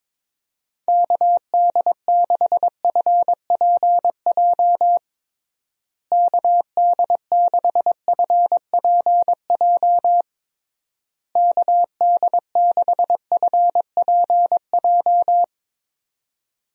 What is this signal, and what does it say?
Morse code, heard as a keyed tone, about 22 wpm, 700 Hz. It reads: KD6FPJ KD6FPJ KD6FPJ
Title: USA Call Signs - 6 Characters - Repeated 3x - 22wpm